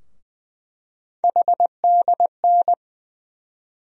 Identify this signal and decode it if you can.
Morse code, heard as a keyed tone, about 20 wpm, 700 Hz. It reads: HDN
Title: USA Call Sign Postfixes - 3 Letters 20wpm